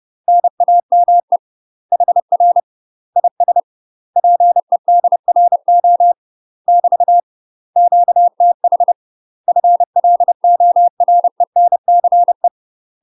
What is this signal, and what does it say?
Morse code, heard as a keyed tone, about 30 wpm, 700 Hz. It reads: NAME HR IS PEDRO <BT> QTH FLORENCE
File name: UK MASTER QSOs TRAINING / 30WPM / UK MASTER QSOs 0046